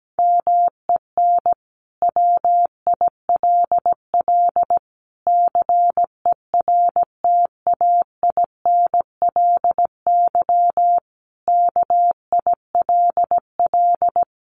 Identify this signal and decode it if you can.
Morse code, heard as a keyed tone, about 17 wpm, 700 Hz. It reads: MEN WILL CERTAINLY KILL
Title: Sentences from Top 1500 Words 17wpm